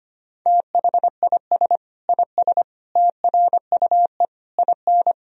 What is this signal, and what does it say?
Morse code, heard as a keyed tone, about 25 wpm, 700 Hz. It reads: THIS IS TRUE IN
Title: Sentences from Top 1000 Words - Review Part 3 - 25wpm